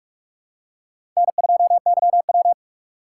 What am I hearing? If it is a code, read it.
Morse code, heard as a keyed tone, about 46 wpm, 700 Hz. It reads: N2YW
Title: USA Call Signs - 4 Characters - 3x Character Spacing - 45wpm